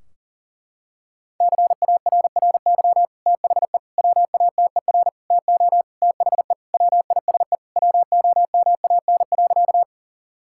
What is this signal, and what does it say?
Morse code, heard as a keyed tone, about 40 wpm, 700 Hz. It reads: CARRY THE WATER TO THE WISE WOMAN.
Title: Sentences from Top 1600 Words 40wpm